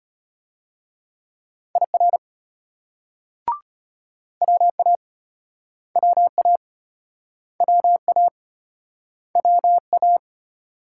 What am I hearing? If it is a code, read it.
Morse code, heard as a keyed tone, about 34 wpm, 700 Hz. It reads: IR WA WA WA WA